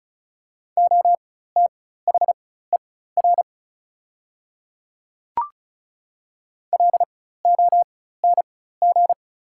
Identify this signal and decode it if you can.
Morse code, heard as a keyed tone, about 36 wpm, spaced as for 16 wpm, 690 Hz. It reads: OTHER LONG
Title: 4x Character Spacing - Top 100 Words - 35wpm